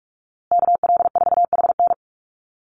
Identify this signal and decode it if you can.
Morse code, heard as a keyed tone, about 44 wpm, 700 Hz. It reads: KL4HN
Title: USA Call Signs Repeated 3x 45wpm